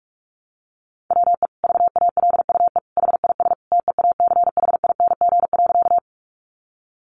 Morse code, heard as a keyed tone, about 44 wpm, 700 Hz. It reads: WE VALUE HIS TEACHING.
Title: Sentences from Top 1100 Words 45wpm